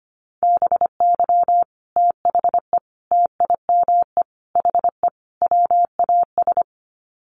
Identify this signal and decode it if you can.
Morse code, heard as a keyed tone, about 25 wpm, 700 Hz. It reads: BY THE TIME HE WAS
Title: Sets of 5 Words Encore 25wpm